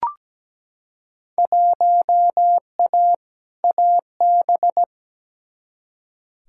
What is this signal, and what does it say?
Morse code, heard as a keyed tone, about 17 wpm, 700 Hz. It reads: 1A AB
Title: Field Day Exchanges - 17wpm